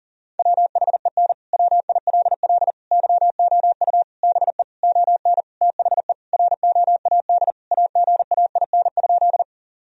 Morse code, heard as a keyed tone, about 40 wpm, 700 Hz. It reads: WHEN WILL YOU BE ON THE ROAD AGAIN?
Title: Sentences from Top 1000 Words - Review Part 2 - 40wpm